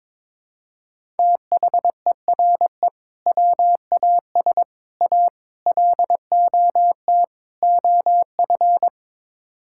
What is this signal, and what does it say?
Morse code, heard as a keyed tone, about 21 wpm, 700 Hz. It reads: THERE WAS A LOT OF
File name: practice / 22wpm / Build A Phrase - 5-Word Phrases - 22wpm